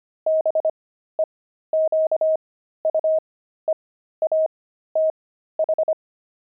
Morse code, heard as a keyed tone, about 25 wpm, spaced as for 13 wpm, 630 Hz. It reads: BEQUEATH
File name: pangram-fs-13-cs-25-jerks